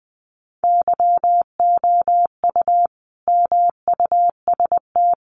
Morse code, heard as a keyed tone, about 19 wpm, 700 Hz. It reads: YOU MUST
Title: Sentences from Top 400 Words 20wpm